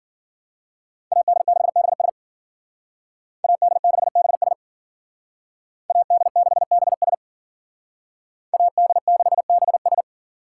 Morse code, heard as a keyed tone, about 49 wpm, 700 Hz. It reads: AD6BS AD6BS AD6BS AD6BS
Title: Speed Racing - USA 2x2 Call Signs 40wpm